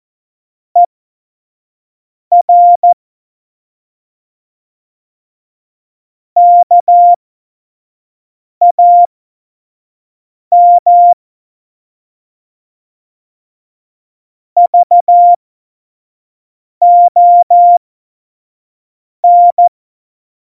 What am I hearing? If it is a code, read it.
Morse code, heard as a keyed tone, about 14 wpm, spaced as for 5 wpm, 700 Hz. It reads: ER KAM VON
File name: Morsekurs / Witze / Witz 27 Juni 17, Tempo 25 F70